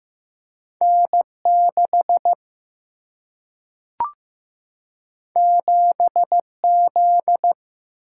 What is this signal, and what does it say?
Morse code, heard as a keyed tone, about 15 wpm, 700 Hz. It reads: N 6 7 Z
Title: Single-Letter Single-Number 15wpm